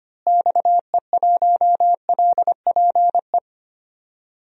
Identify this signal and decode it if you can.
Morse code, heard as a keyed tone, about 25 wpm, 700 Hz. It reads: XE1LPE
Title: DXCC-100 Call Signs - 6 Characters - Repeated 3x - 25wpm